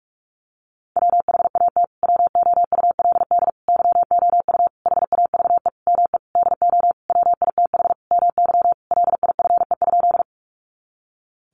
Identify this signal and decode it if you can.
Morse code, heard as a keyed tone, about 45 wpm, 700 Hz. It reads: WHAT WOULD YOU HAVE ME DO WITH MY LIFE?
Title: Sentences from Top 300 Words 45wpm